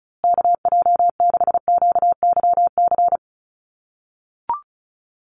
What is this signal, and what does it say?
Morse code, heard as a keyed tone, about 34 wpm, 700 Hz. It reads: KJ6QYC
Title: USA Call Signs Repeated 2x 35wpm